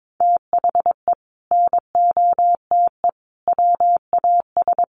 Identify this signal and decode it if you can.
Morse code, heard as a keyed tone, about 22 wpm, 700 Hz. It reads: THE NOTE WAS
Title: Sentences from Top 1500 Words 22wpm